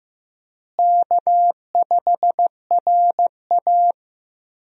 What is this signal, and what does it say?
Morse code, heard as a keyed tone, about 15 wpm, 700 Hz. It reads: K5RA
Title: USA Call Signs - 4 Characters - Repeated 2x - 15wpm